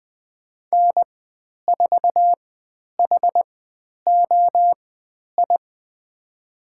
Morse code, heard as a keyed tone, about 20 wpm, spaced as for 10 wpm, 700 Hz. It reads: N4HOI
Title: 109 Lesson 18g - Letter m - Call Signs 20wpm